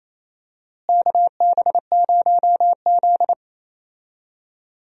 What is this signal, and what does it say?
Morse code, heard as a keyed tone, about 28 wpm, 700 Hz. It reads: KB0Z
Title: USA Call Signs - 4 Characters - Repeated 2x - 28wpm